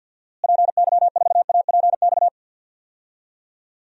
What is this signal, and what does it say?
Morse code, heard as a keyed tone, about 51 wpm, 700 Hz. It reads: PY4APX
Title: DXCC-100 Call Signs - 6 Characters - Repeated 3x - 50wpm